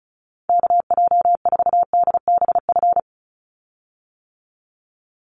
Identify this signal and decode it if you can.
Morse code, heard as a keyed tone, about 34 wpm, 700 Hz. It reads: KJ4DBF
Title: USA Call Signs 35wpm